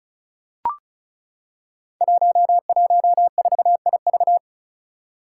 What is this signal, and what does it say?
Morse code, heard as a keyed tone, about 36 wpm, 700 Hz. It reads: 114IV
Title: Five Challenging Letter-Number Combinations 35wpm